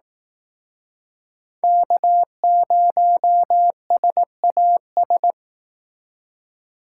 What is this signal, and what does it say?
Morse code, heard as a keyed tone, about 19 wpm, 700 Hz. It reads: K0SAS